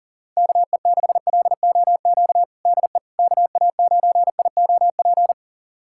Morse code, heard as a keyed tone, about 40 wpm, 700 Hz. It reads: KE6LOQ DE KA9IOP
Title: USA Callsign DE Callsign 40wpm